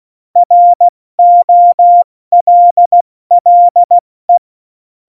Morse code, heard as a keyed tone, about 16 wpm, 700 Hz. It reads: ROLLE
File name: Morsekurs / Witze / Witz 09. April 2019, 70 F80